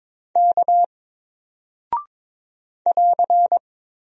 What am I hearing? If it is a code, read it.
Morse code, heard as a keyed tone, about 22 wpm, 700 Hz. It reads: K <AR>
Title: Single Character (Letter:Number:Punctuation:Prosign) Alt 22wpm